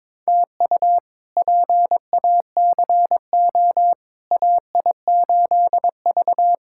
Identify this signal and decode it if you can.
Morse code, heard as a keyed tone, about 22 wpm, 700 Hz. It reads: TU PACO AI8V
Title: K1USN SST QSO Practice - 22wpm